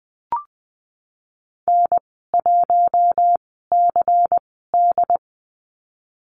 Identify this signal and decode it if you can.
Morse code, heard as a keyed tone, about 19 wpm, spaced as for 14 wpm, 700 Hz. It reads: N1CD
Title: 128 Lesson 21g - Number 6 - Call Signs 20wpm